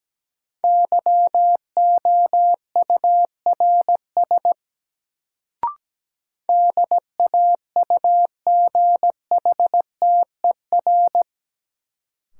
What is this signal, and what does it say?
Morse code, heard as a keyed tone, about 17 wpm, 700 Hz. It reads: YOURS DAUGHTER
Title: Sentences from Top 1500 Words - New Word Drill Alt - 17wpm